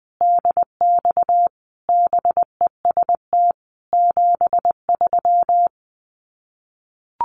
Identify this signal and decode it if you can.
Morse code, heard as a keyed tone, about 19 wpm, 700 Hz. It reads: DX BEST 73
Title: Common QSO Phrases 20wpm